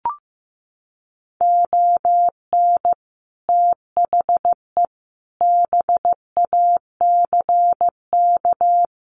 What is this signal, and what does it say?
Morse code, heard as a keyed tone, about 15 wpm, 700 Hz. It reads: ON THE BACK
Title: Sets of 5 Words Encore 15wpm